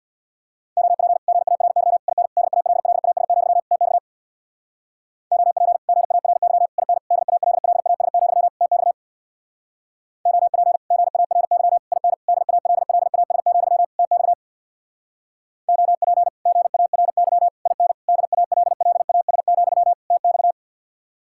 Morse code, heard as a keyed tone, about 63 wpm, 700 Hz. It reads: OP GARY IN DALLAS, TX OP GARY IN DALLAS, TX OP GARY IN DALLAS, TX OP GARY IN DALLAS, TX